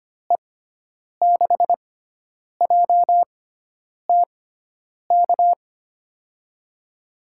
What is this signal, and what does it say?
Morse code, heard as a keyed tone, about 25 wpm, spaced as for 9 wpm, 700 Hz. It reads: E6JTK
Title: USA Call Signs - 6 Characters - 6x Character Spacing - 25wpm